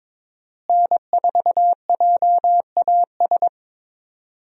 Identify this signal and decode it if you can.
Morse code, heard as a keyed tone, about 22 wpm, 700 Hz. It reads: N4JAS
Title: USA Call Signs - 5 Characters - Repeated 3x - 22wpm